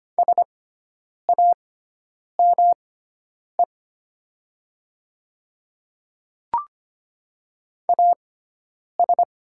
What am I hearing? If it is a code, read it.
Morse code, heard as a keyed tone, about 25 wpm, spaced as for 9 wpm, 700 Hz. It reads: SAME AS
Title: 6x Character Spacing - Top 200 Words - 25wpm